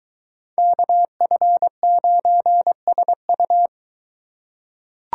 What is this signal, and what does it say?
Morse code, heard as a keyed tone, about 23 wpm, 700 Hz. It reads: KF9SU